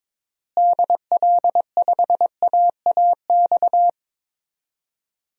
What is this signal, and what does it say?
Morse code, heard as a keyed tone, about 22 wpm, 700 Hz. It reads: DL5AAX
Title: DXCC-100 Call Signs - 6 Characters - Repeated 2x - 22wpm